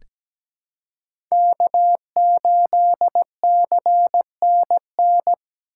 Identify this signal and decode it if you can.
Morse code, heard as a keyed tone, about 17 wpm, 700 Hz. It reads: K8CNN
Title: USA Call Signs - 5 Characters - 2x Character Spacing - 17wpm